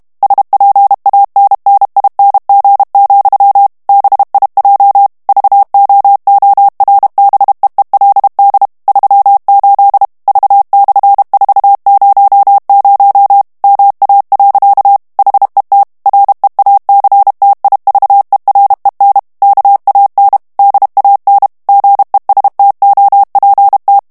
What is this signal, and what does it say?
Morse code, heard as a keyed tone, about 32 wpm, 800 Hz. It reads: SPANNING, BIJ VOORBEELD 38 V/400 MA. HET REACTIVEREN KAN DAN GESTOPT
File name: HOLLANDAIS texte clair 2 vitesse 30 mots minute